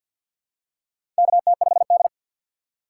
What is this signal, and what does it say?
Morse code, heard as a keyed tone, about 52 wpm, 700 Hz. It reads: KT5D
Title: USA Call Signs - 4 Characters - 3x Character Spacing - 50wpm